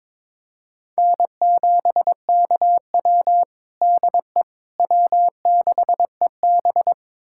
Morse code, heard as a keyed tone, about 21 wpm, 700 Hz. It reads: N7KW DE W6EB
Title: USA Callsign DE Callsign - 4 Characters - 22wpm